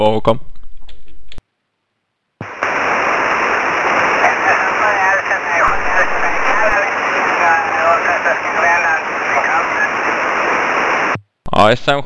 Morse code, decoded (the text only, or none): none